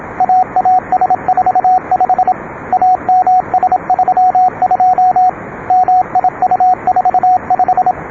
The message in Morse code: AAS45 AMS32 MIU45